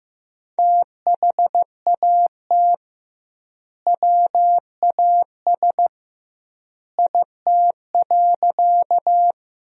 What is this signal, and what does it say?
Morse code, heard as a keyed tone, about 15 wpm, 700 Hz. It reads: THAT WAS IT.
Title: 2x Word-Spacing - Sentences from Top 100 Words - 15wpm